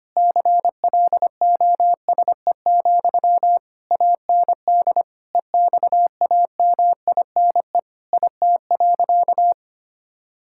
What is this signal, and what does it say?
Morse code, heard as a keyed tone, about 25 wpm, 700 Hz. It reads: CLOSE, AND EXAMINE IT.
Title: Sentences from Top 1600 Words 25wpm